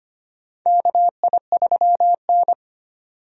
Morse code, heard as a keyed tone, about 25 wpm, 700 Hz. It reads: KI3N
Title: USA Call Signs - 4 Characters - 25wpm